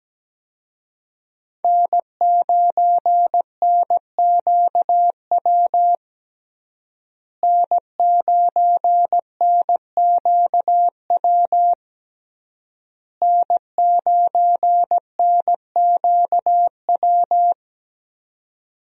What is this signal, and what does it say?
Morse code, heard as a keyed tone, about 17 wpm, 700 Hz. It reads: N9NQW N9NQW N9NQW
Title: USA Call Signs - 5 Characters - Repeated 3x - 17wpm